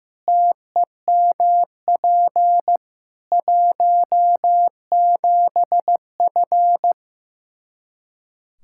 Morse code, heard as a keyed tone, about 15 wpm, 700 Hz. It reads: TEMP 17F
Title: Practice QSO - Exchange 2 - Info - 15wpm